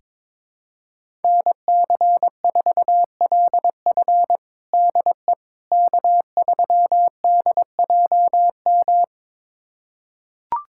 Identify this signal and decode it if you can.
Morse code, heard as a keyed tone, about 21 wpm, 700 Hz. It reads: NC4LF DE K3DJM
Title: USA Callsign DE Callsign - 5 Characters - 22wpm